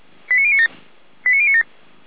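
Five-tone selective call (ZVEI), heard as quick runs of five tones, tones of about 80 ms. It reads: 89097 89097
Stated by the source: ZVEI five-tone selective call received on an SDR.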